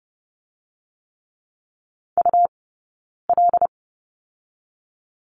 Morse code, heard as a keyed tone, about 30 wpm, 700 Hz. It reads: U L
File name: course / 30:10wpm / 099 Lesson 17d - Letter u - Two Characters 30wpm